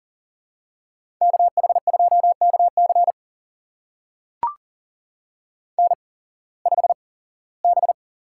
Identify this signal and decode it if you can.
Morse code, heard as a keyed tone, about 40 wpm, spaced as for 21 wpm, 700 Hz. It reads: KH2KC N5B
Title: USA Call Signs - 5 Characters - 8x Character Spacing - 40wpm